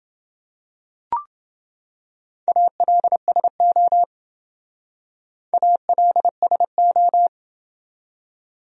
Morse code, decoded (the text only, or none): ALSO ALSO